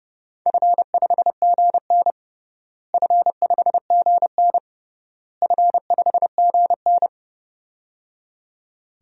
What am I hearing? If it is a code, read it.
Morse code, heard as a keyed tone, about 30 wpm, 700 Hz. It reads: F5GN F5GN F5GN
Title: DXCC-100 Call Signs - 4 Characters - Repeated 3x - 30wpm